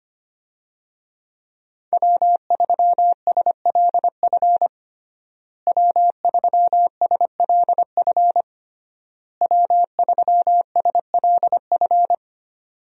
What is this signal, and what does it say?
Morse code, heard as a keyed tone, about 25 wpm, 700 Hz. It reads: W3SLF W3SLF W3SLF
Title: USA Call Signs - 5 Characters - Repeated 3x - 25wpm